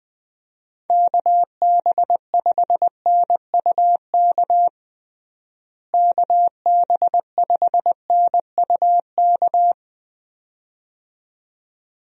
Morse code, heard as a keyed tone, about 20 wpm, 700 Hz. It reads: KB5NUK KB5NUK
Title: USA Call Signs - 6 Characters - Repeated 2x - 20wpm